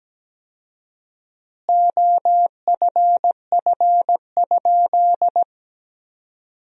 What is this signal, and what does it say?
Morse code, heard as a keyed tone, about 17 wpm, 700 Hz. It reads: OFF?
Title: 4x Word-Spacing - Sentences from Top 500 Words - 17wpm